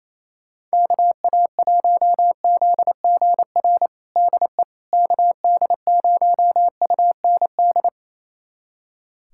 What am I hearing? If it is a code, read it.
Morse code, heard as a keyed tone, about 27 wpm, 700 Hz. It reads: KA1ZGR DE KD0UND
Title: USA Callsign DE Callsign - 6 Characters - 28wpm